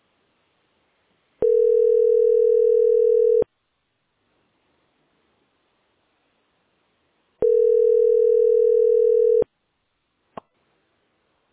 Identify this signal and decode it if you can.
Ringback tone (North America), 440 + 480 Hz.